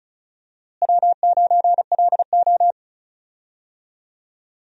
Morse code, heard as a keyed tone, about 36 wpm, 700 Hz. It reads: W9LO
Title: USA Call Signs - 4 Characters - Repeated 2x - 35wpm